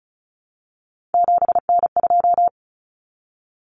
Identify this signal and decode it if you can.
Morse code, heard as a keyed tone, about 34 wpm, 700 Hz. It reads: 7N2
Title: Common Cut Numbers - 3-Digit 35wpm